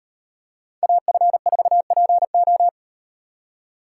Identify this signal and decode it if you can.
Morse code, heard as a keyed tone, about 38 wpm, 700 Hz. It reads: AF4PO